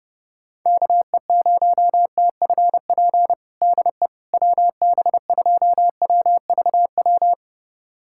Morse code, heard as a keyed tone, about 30 wpm, 690 Hz. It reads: KE0TFP DE WB2WVW
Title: USA Callsign DE Callsign - 6 Characters - 30wpm